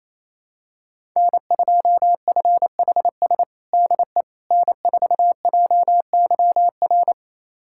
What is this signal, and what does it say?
Morse code, heard as a keyed tone, about 28 wpm, 700 Hz. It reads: N2FHS DE N4JYR
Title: USA Callsign DE Callsign - 5 Characters - 28wpm